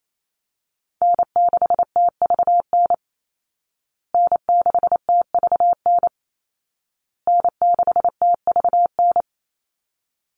Morse code, heard as a keyed tone, about 28 wpm, 700 Hz. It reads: N6TVN N6TVN N6TVN
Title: Call Signs Repeated 2x - 28wpm